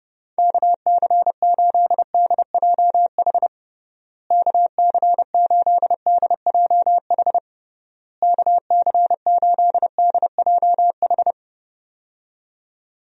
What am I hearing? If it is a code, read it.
Morse code, heard as a keyed tone, about 30 wpm, 700 Hz. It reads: KC8DJH KC8DJH KC8DJH
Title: USA Call Signs - 6 Characters - Repeated 3x - 30wpm